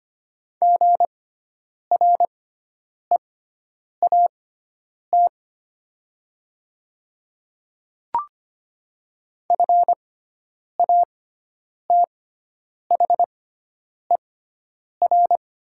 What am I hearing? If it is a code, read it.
Morse code, heard as a keyed tone, about 25 wpm, spaced as for 9 wpm, 700 Hz. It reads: GREAT FATHER
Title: 6x Character Spacing - Top 200 Words - 25wpm